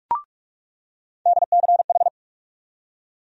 Morse code, heard as a keyed tone, about 47 wpm, 700 Hz. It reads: DCH
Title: USA Call Sign Postfixes - 3 Letters 45wpm